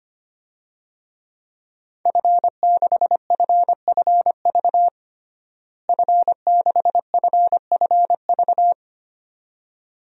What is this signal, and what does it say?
Morse code, heard as a keyed tone, about 25 wpm, 700 Hz. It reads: F6FFV F6FFV
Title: DXCC-100 Call Signs - 5 Characters - Repeated 2x - 25wpm